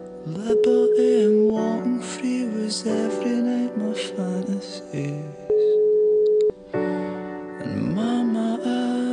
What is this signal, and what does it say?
Ringback tone (Europe, 425 Hz), with noise.